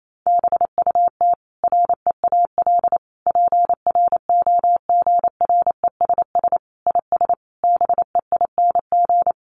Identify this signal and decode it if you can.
Morse code, heard as a keyed tone, about 28 wpm, 700 Hz. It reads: BUT REAL PROGRESS IS BEING
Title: Sentences from Top 1400 Words 28wpm